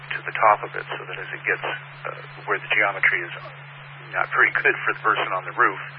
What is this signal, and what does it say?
Emergency Alert System (SAME) tones, heard as harsh buzzing bursts, with speech. It is heard from 1.2 to 4.8 s.